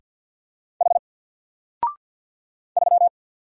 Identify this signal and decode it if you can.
Morse code, heard as a keyed tone, about 49 wpm, 700 Hz. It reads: H 3
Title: Single Character (Letter:Number:Punctuation:Prosign) 50wpm